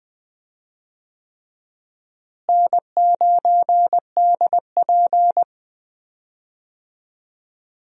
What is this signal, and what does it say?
Morse code, heard as a keyed tone, about 20 wpm, 700 Hz. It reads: N9DP
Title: USA Call Signs - 4 Characters - Repeated 3x - 20wpm